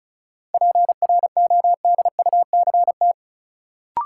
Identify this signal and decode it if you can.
Morse code, heard as a keyed tone, about 36 wpm, 700 Hz. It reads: PRODUCT
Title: Sentences from Top 1000 Words - Review Part 2 - New Word Drill - 35wpm